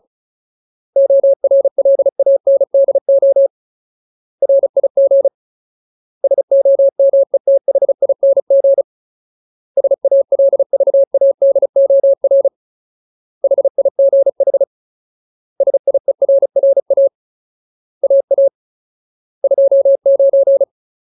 Morse code, heard as a keyed tone, about 34 wpm, 550 Hz. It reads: ORLANDO RIG SOMETHING SALVADOR HIGH SIERRA AA 29
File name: Ultimate Word List / 35WPM Ultimate